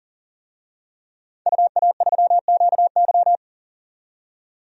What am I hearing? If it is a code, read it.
Morse code, heard as a keyed tone, about 40 wpm, 700 Hz. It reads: UA3QY
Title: DXCC-100 Call Signs - 5 Characters - 40wpm